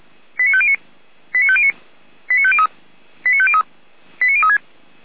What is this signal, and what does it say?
Five-tone selective call (ZVEI), heard as quick runs of five tones, tones of about 70 ms. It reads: 89509 89509 89503 89503 89036
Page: /455/SDR/ZVEI/89010